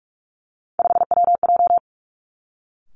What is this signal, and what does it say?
Morse code, heard as a keyed tone, about 46 wpm, 700 Hz. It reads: 5WJ
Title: Three Challenging Letter-Number Combinations 45wpm